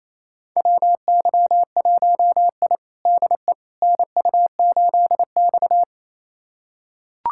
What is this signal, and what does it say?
Morse code, heard as a keyed tone, about 28 wpm, 700 Hz. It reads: WY1I DE NU8X
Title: USA Callsign DE Callsign - 4 Characters - 28wpm